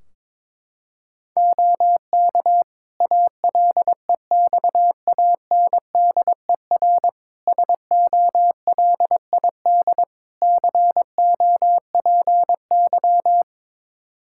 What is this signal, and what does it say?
Morse code, heard as a keyed tone, about 22 wpm, 700 Hz. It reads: OK ALEXANDER SOLID COPY
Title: K1USN SST QSO Practice 22wpm